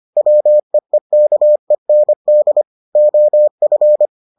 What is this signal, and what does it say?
Morse code, heard as a keyed tone, about 25 wpm, 600 Hz. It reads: WEEKEND OF